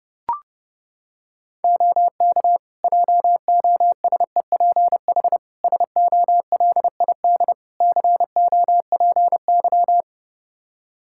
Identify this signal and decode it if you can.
Morse code, heard as a keyed tone, about 30 wpm, 700 Hz. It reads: OK JOSEPH SOLID COPY
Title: Practice QSO - Exchange 3 - More Info - 30wpm